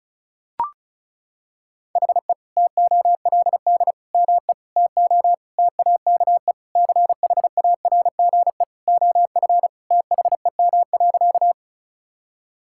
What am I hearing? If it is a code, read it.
Morse code, heard as a keyed tone, about 36 wpm, 700 Hz. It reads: HE TOLD ME TO TAKE CHARGE OF THEM.
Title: Sentences from Top 1000 Words - Review Part 6 - 35wpm